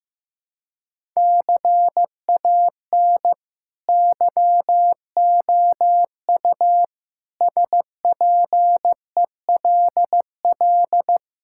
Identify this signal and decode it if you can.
Morse code, heard as a keyed tone, about 15 wpm, 700 Hz. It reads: CAN YOU SPELL
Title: Sentences from Top 1000 Words - Review Part 5 - 15wpm